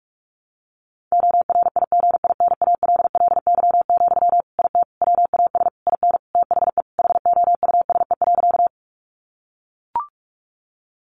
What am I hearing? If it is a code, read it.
Morse code, heard as a keyed tone, about 44 wpm, 700 Hz. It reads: ORIGINALLY, IT WAS IN THE HOUSE.
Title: Sentences from Top 1500 Words 45wpm